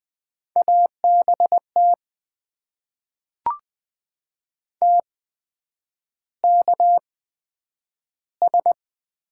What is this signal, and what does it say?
Morse code, heard as a keyed tone, about 20 wpm, spaced as for 9 wpm, 700 Hz. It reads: ABT TKS
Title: Top 90 QSO Elements - 8x Character Spacing - 20wpm